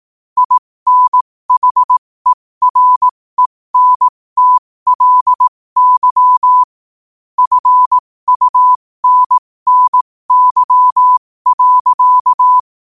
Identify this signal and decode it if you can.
Morse code, heard as a keyed tone, about 18 wpm, spaced as for 15 wpm, 1000 Hz. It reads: INHERENTLY FUNNY.
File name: qotdfiles / 15wpm